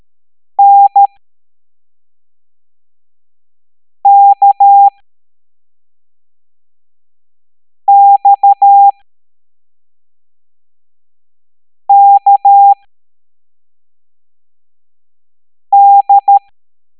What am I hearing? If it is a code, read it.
Morse code, heard as a keyed tone, about 13 wpm, spaced as for 3 wpm, 800 Hz. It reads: NKXKD